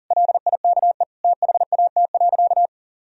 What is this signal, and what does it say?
Morse code, heard as a keyed tone, about 40 wpm, 700 Hz. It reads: LIKE THAT.
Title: Sentences from Top 1000 Words - Review Part 1 - 40wpm